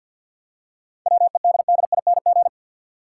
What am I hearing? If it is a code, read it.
Morse code, heard as a keyed tone, about 49 wpm, 700 Hz. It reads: WEDDING
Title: Seven Letter Words - 50wpm